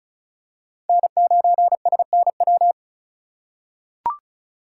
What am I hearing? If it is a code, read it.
Morse code, heard as a keyed tone, about 36 wpm, 700 Hz. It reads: N9SNW
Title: USA Call Signs - 5 Characters - 2x Character Spacing - 35wpm